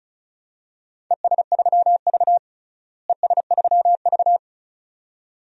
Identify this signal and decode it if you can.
Morse code, heard as a keyed tone, about 34 wpm, 700 Hz. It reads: ES3V ES3V
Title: DXCC-100 Call Signs - 4 Characters - Repeated 2x - 35wpm